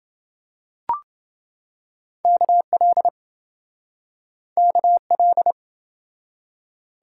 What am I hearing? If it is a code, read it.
Morse code, heard as a keyed tone, about 27 wpm, 700 Hz. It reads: KL KL